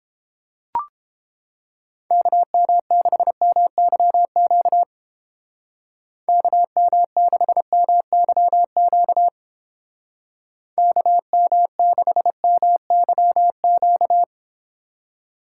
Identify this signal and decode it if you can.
Morse code, heard as a keyed tone, about 30 wpm, 700 Hz. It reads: KM6MYQ KM6MYQ KM6MYQ